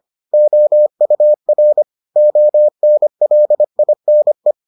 Morse code, heard as a keyed tone, about 25 wpm, 600 Hz. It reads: OUR ONLINE